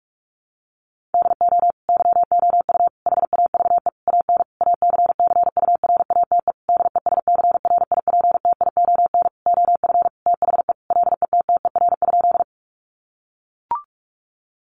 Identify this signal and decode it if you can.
Morse code, heard as a keyed tone, about 44 wpm, 700 Hz. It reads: DO YOU HAVE AN ACCURATE DESCRIPTION OF THE LETTER?
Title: Sentences from Top 1300 Words 45wpm